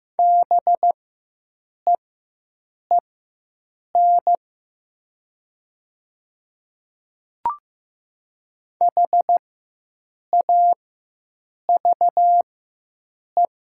Morse code, heard as a keyed tone, about 15 wpm, spaced as for 7 wpm, 700 Hz. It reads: BEEN HAVE
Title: 4x Character Spacing - Top 100 Words - 15wpm